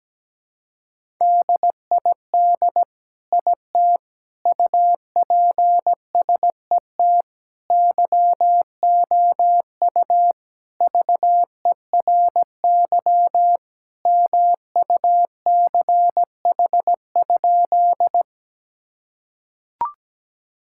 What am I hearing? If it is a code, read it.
Morse code, heard as a keyed tone, about 17 wpm, 700 Hz. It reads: DID IT UPSET YOU VERY MUCH?
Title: Sentences from Top 1000 Words - Review Part 5 - 17wpm